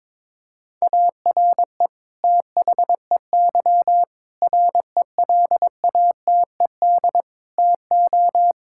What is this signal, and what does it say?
Morse code, heard as a keyed tone, about 21 wpm, 700 Hz. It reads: ARE THEY RELATED TO
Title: Sentences from Top 1000 Words - Review Part 5 - 22wpm